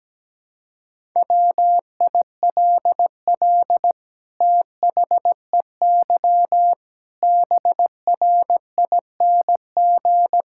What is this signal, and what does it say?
Morse code, heard as a keyed tone, about 17 wpm, 700 Hz. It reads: WILL THEY BRING
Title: Sentences from Top 1000 Words - Review Part 6 - 17wpm